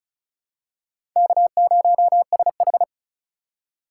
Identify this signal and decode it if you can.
Morse code, heard as a keyed tone, about 36 wpm, 700 Hz. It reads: K0SH
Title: USA Call Signs - 4 Characters - 4x Character Spacing - 35wpm